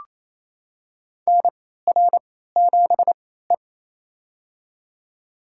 Morse code, heard as a keyed tone, about 28 wpm, spaced as for 16 wpm, 700 Hz. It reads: NR7E
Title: USA Call Signs - 4 Characters - 3x Character Spacing - 28wpm